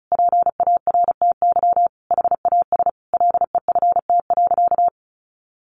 Morse code, heard as a keyed tone, about 34 wpm, 700 Hz. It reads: PARTY HAS LEFT.
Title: Sentences from Top 1500 Words 35wpm